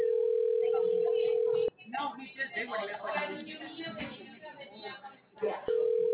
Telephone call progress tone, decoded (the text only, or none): ringback tone